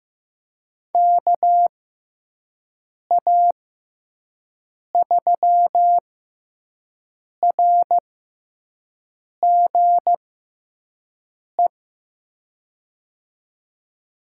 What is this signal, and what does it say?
Morse code, heard as a keyed tone, about 15 wpm, spaced as for 5 wpm, 700 Hz. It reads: KA3RGE
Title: USA Call Signs - 6x Character Spacing - 15wpm